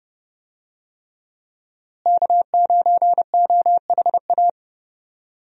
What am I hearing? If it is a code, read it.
Morse code, heard as a keyed tone, about 30 wpm, 700 Hz. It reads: K9OHA
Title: USA Call Signs - 5 Characters - Repeated 3x - 30wpm